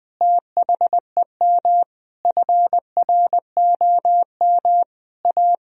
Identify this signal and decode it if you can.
Morse code, heard as a keyed tone, about 20 wpm, 700 Hz. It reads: THEM FROM A
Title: Sentences from Top 1000 Words - Review Part 3 - 20wpm